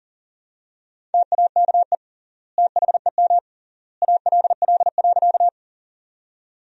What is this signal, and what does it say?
Morse code, heard as a keyed tone, about 40 wpm, 700 Hz. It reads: TAKE THEM ALL.
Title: 3x Word-Spacing - Sentences from Top 200 Words - 40wpm